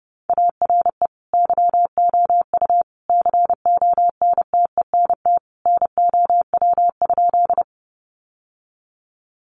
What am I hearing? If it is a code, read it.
Morse code, heard as a keyed tone, about 30 wpm, 700 Hz. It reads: ARE YOU CONTENT NOW?
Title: Sentences from Top 900 Words 30wpm